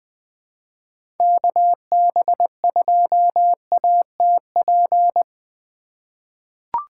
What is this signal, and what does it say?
Morse code, heard as a keyed tone, about 20 wpm, 700 Hz. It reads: KB2ATP
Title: USA Call Signs - 6 Characters - 3x Character Spacing - 20wpm